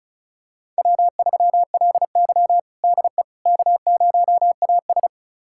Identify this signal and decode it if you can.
Morse code, heard as a keyed tone, about 36 wpm, 700 Hz. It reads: W3LY DE K0AS
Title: USA Callsign DE Callsign - 4 Characters - 35wpm